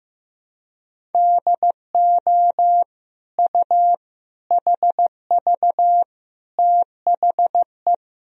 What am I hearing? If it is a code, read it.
Morse code, heard as a keyed tone, about 15 wpm, 700 Hz. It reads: DO U HV THE
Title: Sentences from Top 300 Words and Abbreviations 15wpm